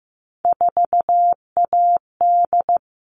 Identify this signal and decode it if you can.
Morse code, heard as a keyed tone, about 15 wpm, 700 Hz. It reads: 4AD
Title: All Cut Numbers - 3-Digit 15wpm